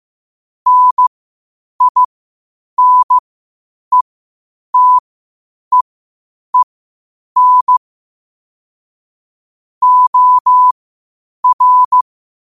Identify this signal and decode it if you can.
Morse code, heard as a keyed tone, about 15 wpm, spaced as for 8 wpm, 1000 Hz. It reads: NINETEEN OR
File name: qotdfiles / 8wpm